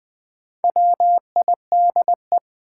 Morse code, heard as a keyed tone, about 20 wpm, 700 Hz. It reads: WIDE
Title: Sets of 4 Words - New Word Drill - 20wpm